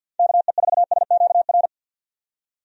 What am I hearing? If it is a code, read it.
Morse code, heard as a keyed tone, about 49 wpm, 700 Hz. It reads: KE4SQR
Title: USA Call Signs - 6 Characters - 50wpm